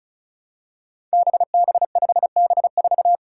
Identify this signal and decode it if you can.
Morse code, heard as a keyed tone, about 34 wpm, 700 Hz. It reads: BB5B4
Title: Five Challenging Letter-Number Combinations 35wpm